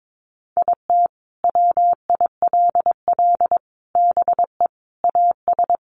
Morse code, heard as a keyed tone, about 22 wpm, 700 Hz. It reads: IT WILL BE AS
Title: Sentences from Top 1400 Words 22wpm